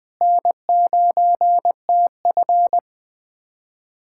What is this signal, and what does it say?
Morse code, heard as a keyed tone, about 20 wpm, 700 Hz. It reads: N9TF
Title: USA Callsign DE Callsign - 4 Characters - 20wpm